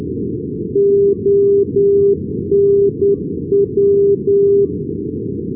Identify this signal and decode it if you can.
Morse code, heard as a keyed tone, about 10 wpm, 400 Hz. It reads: ONW
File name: europe / bel / bel-254 ndb antwerpen onw-355-ac,d-2011-rb